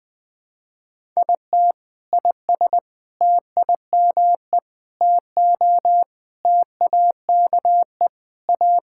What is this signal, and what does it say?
Morse code, heard as a keyed tone, about 20 wpm, 700 Hz. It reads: IT IS TIME TO TAKE A
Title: Sentences with 4-Word Phrases - 20wpm